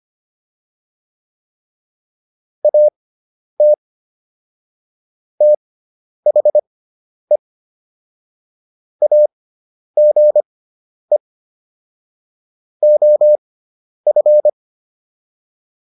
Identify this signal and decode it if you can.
Morse code, heard as a keyed tone, about 25 wpm, spaced as for 10 wpm, 600 Hz. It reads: AT THE AGE OF